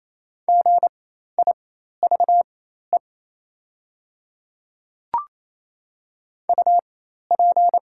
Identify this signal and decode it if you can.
Morse code, heard as a keyed tone, about 28 wpm, spaced as for 13 wpm, 700 Hz. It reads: GIVE UP
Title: 4x Character Spacing - Top 200 Words - 28wpm